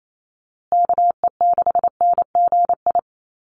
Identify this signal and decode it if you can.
Morse code, heard as a keyed tone, about 28 wpm, 700 Hz. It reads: KE6NGI
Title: USA Call Signs Repeated 2x 28wpm